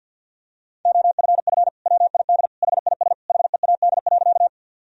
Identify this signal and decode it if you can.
Morse code, heard as a keyed tone, about 52 wpm, 700 Hz. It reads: OFF WID HIS HEAD.
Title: Sentences from Top 300 Words and Abbreviations 50wpm